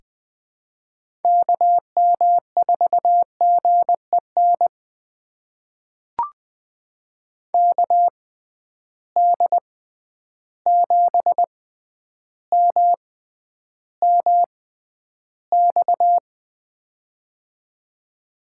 Morse code, heard as a keyed tone, about 21 wpm, spaced as for 10 wpm, 700 Hz. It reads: KM4GEN KD7MMX
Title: USA Call Signs - 6 Characters - 6x Character Spacing - 20wpm